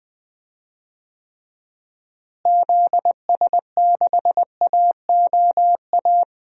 Morse code, heard as a keyed tone, about 21 wpm, 700 Hz. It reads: ZS6AOA
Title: DXCC-100 Call Signs - 6 Characters - Repeated 3x - 20wpm